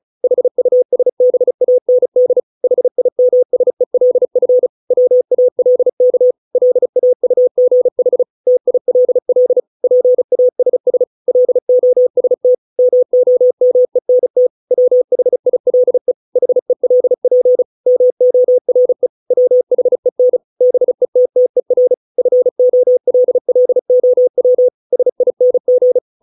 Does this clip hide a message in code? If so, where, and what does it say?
Yes: Morse code from 0.2 to 26.0 s, heard as a keyed tone, about 34 wpm, 500 Hz. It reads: HUSBAND HIMSELF WALK LAUGH TILL PASS LOST MOMENT WHILE HELP MORE WHEN BETTER FOLLOW SING